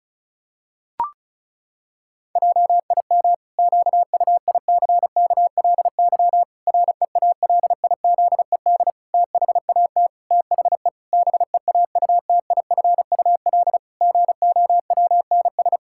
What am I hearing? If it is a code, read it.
Morse code, heard as a keyed tone, about 36 wpm, 700 Hz. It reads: JIM QUICKLY REALIZED THAT THE BEAUTIFUL GOWNS
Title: Pangram Challenge v2 - 35wpm